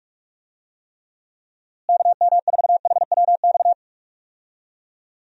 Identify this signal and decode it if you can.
Morse code, heard as a keyed tone, about 44 wpm, 700 Hz. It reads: KM4HWX
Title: USA Call Signs - 6 Characters - Repeated 2x - 45wpm